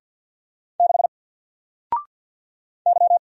Morse code, heard as a keyed tone, about 50 wpm, 700 Hz. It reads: 6 Y
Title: Single Letter-Number 50wpm